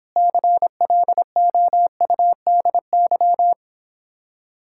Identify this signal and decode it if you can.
Morse code, heard as a keyed tone, about 31 wpm, 700 Hz. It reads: CLOUDY